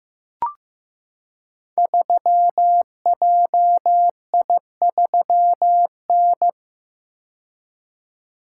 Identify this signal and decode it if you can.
Morse code, heard as a keyed tone, about 15 wpm, 700 Hz. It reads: 3JI3N
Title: Code Groups - 5 Letter:Numbers - 15wpm